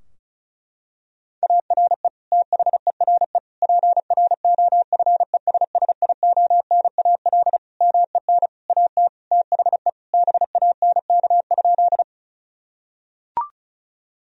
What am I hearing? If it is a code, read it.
Morse code, heard as a keyed tone, about 36 wpm, 700 Hz. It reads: ARE THERE PROFESSIONAL MEN AT THE BANK?
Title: Sentences from Top 1000 Words - Review Part 4 - 35wpm